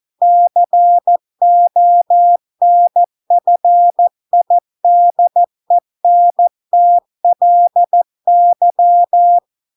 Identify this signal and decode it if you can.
Morse code, heard as a keyed tone, about 14 wpm, 700 Hz. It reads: CONFIDENTLY